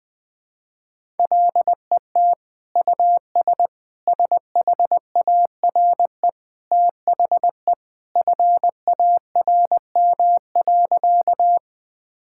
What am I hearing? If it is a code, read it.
Morse code, heard as a keyed tone, about 21 wpm, 700 Hz. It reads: LET US SHARE THE FARM.
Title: Sentences from Top 1000 Words - Review Part 6 - 20wpm